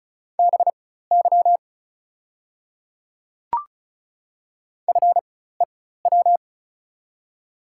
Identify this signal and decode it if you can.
Morse code, heard as a keyed tone, about 36 wpm, spaced as for 16 wpm, 700 Hz. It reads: BY FEW
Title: 4x Character Spacing - Top 300 Words - 35wpm